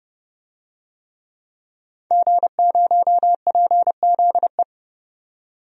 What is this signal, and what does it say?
Morse code, heard as a keyed tone, about 30 wpm, 700 Hz. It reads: G0PZE
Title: DXCC-100 Call Signs - 5 Characters - 30wpm